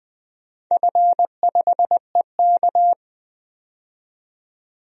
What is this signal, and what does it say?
Morse code, heard as a keyed tone, about 20 wpm, 700 Hz. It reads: F5EK
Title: DXCC-100 Call Signs - 4 Characters - 20wpm